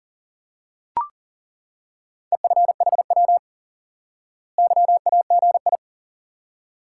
Morse code, heard as a keyed tone, about 37 wpm, 700 Hz. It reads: EFHW YAGI
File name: IQE Course 3 - Lesson 89 - YAGI - Two Elements - 40wpm